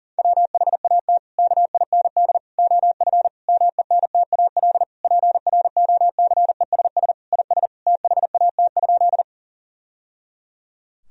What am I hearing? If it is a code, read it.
Morse code, heard as a keyed tone, about 40 wpm, 700 Hz. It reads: WHAT KIND OF MENTAL PROCESS IS THAT?
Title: Sentences from Top 1000 Words - Review Part 4 - 40wpm